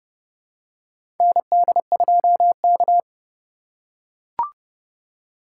Morse code, heard as a keyed tone, about 29 wpm, 700 Hz. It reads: ND2K
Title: USA Call Signs - 4 Characters - 3x Character Spacing - 30wpm